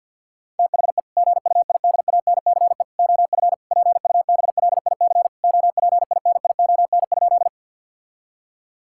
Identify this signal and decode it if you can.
Morse code, heard as a keyed tone, about 50 wpm, 700 Hz. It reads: THE GUIDANCE OF PUBLIC OPINION?
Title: Sentences from Top 1600 Words 50wpm